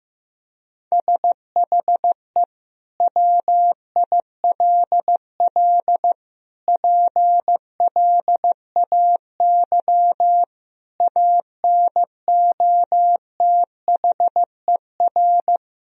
Morse code, heard as a keyed tone, about 15 wpm, 700 Hz. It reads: SHE WILL PLAY ANOTHER
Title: Sentences from Top 1000 Words - Review Part 4 - 15wpm